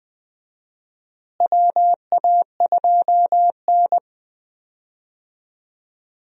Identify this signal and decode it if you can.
Morse code, heard as a keyed tone, about 20 wpm, 700 Hz. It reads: WA2N
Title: USA Call Signs - 4 Characters - Repeated 2x - 20wpm